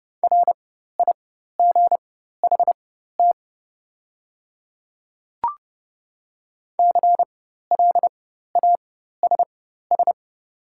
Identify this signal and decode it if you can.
Morse code, heard as a keyed tone, about 30 wpm, spaced as for 14 wpm, 700 Hz. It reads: RIGHT CLASS
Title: 4x Character Spacing - Top 500 Words - 30wpm